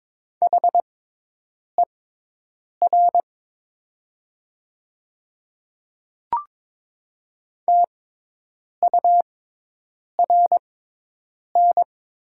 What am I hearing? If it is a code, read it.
Morse code, heard as a keyed tone, about 22 wpm, spaced as for 8 wpm, 700 Hz. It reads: HER TURN
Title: 6x Character Spacing - Top 200 Words - 22wpm